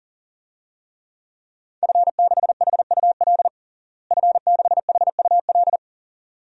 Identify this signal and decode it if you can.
Morse code, heard as a keyed tone, about 40 wpm, 700 Hz. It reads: F6HUL F6HUL
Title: DXCC-100 Call Signs - 5 Characters - Repeated 2x - 40wpm